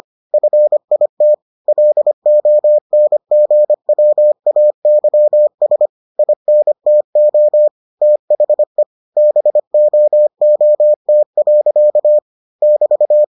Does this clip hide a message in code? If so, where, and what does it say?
Yes: Morse code from 0.3 to 13.2 s, heard as a keyed tone, about 25 wpm, 600 Hz. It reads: FIT LONGWAYS INTO THE BOOT. <BT>